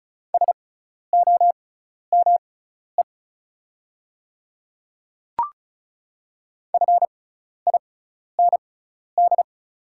Morse code, heard as a keyed tone, about 36 wpm, spaced as for 12 wpm, 690 Hz. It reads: SOME FIND
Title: 6x Character Spacing - Top 100 Words - 35wpm